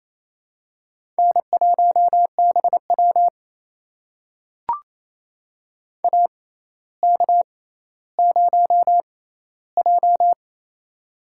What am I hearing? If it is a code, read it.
Morse code, heard as a keyed tone, about 28 wpm, spaced as for 14 wpm, 700 Hz. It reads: N1BW AK0J
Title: USA Call Signs - 4 Characters - 6x Character Spacing - 28wpm